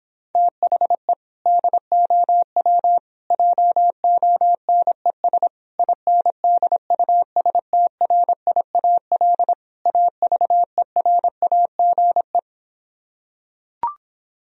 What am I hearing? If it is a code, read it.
Morse code, heard as a keyed tone, about 26 wpm, 700 Hz. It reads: THE DOW JONES INDUSTRIAL AVERAGE